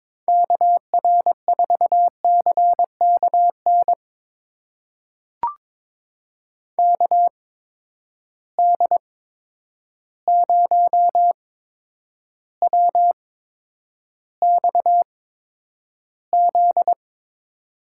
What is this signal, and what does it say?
Morse code, heard as a keyed tone, about 22 wpm, spaced as for 9 wpm, 700 Hz. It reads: KR4CKN KD0WXZ
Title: USA Call Signs - 6 Characters - 8x Character Spacing - 22wpm